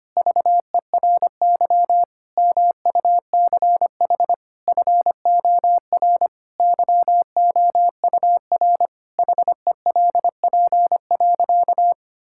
Morse code, heard as a keyed tone, about 25 wpm, 700 Hz. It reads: VERY MUCH FOR YOUR HELP.
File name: practice / 25wpm / Sentences with 4-Word Phrases - 25wpm